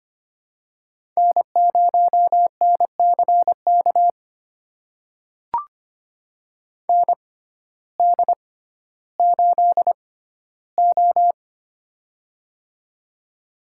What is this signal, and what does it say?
Morse code, heard as a keyed tone, about 25 wpm, spaced as for 14 wpm, 700 Hz. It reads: N0NCK ND8O
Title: USA Call Signs - 6x Character Spacing - 25wpm